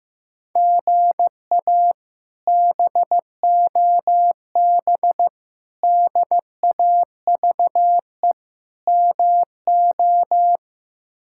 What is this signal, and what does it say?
Morse code, heard as a keyed tone, about 15 wpm, 700 Hz. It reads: GA BOB DAVE MO
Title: K1USN SST QSO Practice - 15wpm